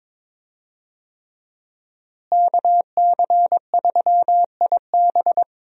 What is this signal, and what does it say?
Morse code, heard as a keyed tone, about 22 wpm, 700 Hz. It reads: KC3IB
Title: USA Call Signs - 5 Characters - Repeated 3x - 22wpm